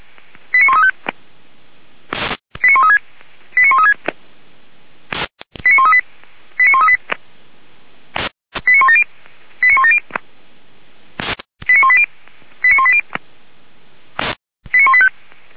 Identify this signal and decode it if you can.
Five-tone selective call (ZVEI), heard as quick runs of five tones, tones of about 80 ms. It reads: 89135 89136 89136 89138 89138 89170 89170 89180 89180 89185